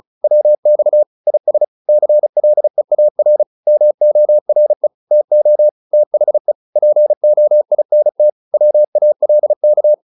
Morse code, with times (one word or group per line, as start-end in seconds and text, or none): 0.2-1.0 WX
1.3-1.6 IS
1.9-3.4 CLEAR
3.7-4.9 MORE
5.1-5.7 TO
5.9-6.5 THE
6.8-8.3 POINT
8.5-9.9 WALK